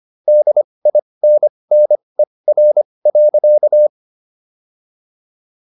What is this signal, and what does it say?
Morse code, heard as a keyed tone, about 25 wpm, spaced as for 20 wpm, 600 Hz. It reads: DINNER.